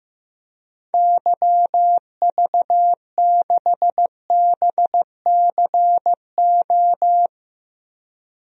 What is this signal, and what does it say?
Morse code, heard as a keyed tone, about 16 wpm, 700 Hz. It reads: YV6BCO
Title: DXCC-100 Call Signs - 6 Characters - 15wpm